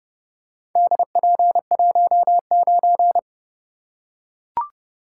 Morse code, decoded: DP19

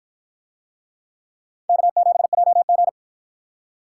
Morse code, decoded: K7JG

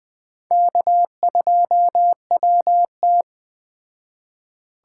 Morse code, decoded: K2WT

